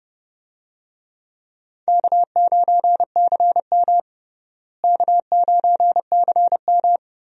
Morse code, heard as a keyed tone, about 30 wpm, 700 Hz. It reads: K9CM K9CM
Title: USA Call Signs - 4 Characters - Repeated 2x - 30wpm